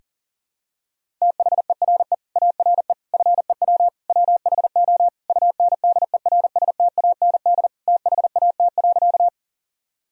Morse code, heard as a keyed tone, about 37 wpm, 700 Hz. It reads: THERE ARE FEW WHO UNDERSTAND THAT.
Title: Sentences from Top 1000 Words - Review Part 1 - 40wpm